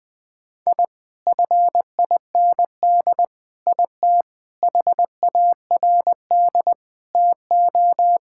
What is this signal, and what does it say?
Morse code, heard as a keyed tone, about 20 wpm, 700 Hz. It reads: I FIND IT HARD TO
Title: Sentences with 7-Word Phrases - Spoken Beginning - 20wpm